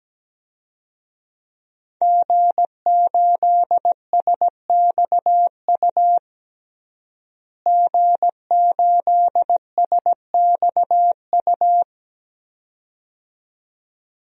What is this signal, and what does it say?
Morse code, heard as a keyed tone, about 17 wpm, 700 Hz. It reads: G8SXU G8SXU
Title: DXCC-100 Call Signs - 5 Characters - Repeated 2x - 17wpm